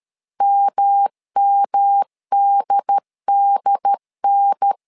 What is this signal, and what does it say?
Morse code, heard as a keyed tone, about 13 wpm, 800 Hz. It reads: MMDDN
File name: LEZIONI / 3 MWRO / 3 MW 12wpm 1633